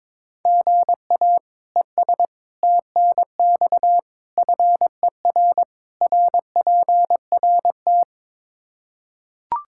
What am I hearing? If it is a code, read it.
Morse code, heard as a keyed tone, about 22 wpm, 700 Hz. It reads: GA ES TNX FER RPRT
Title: K1USN SST QSO Practice 22wpm